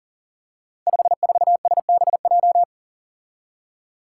Morse code, heard as a keyed tone, about 42 wpm, 700 Hz. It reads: 54SBJ
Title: Five Challenging Letter-Number Combinations 40wpm